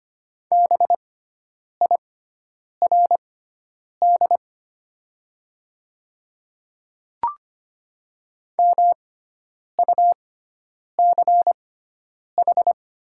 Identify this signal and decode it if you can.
Morse code, heard as a keyed tone, about 25 wpm, spaced as for 9 wpm, 700 Hz. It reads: BIRD MUCH
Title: 6x Character Spacing - Top 500 Words - 25wpm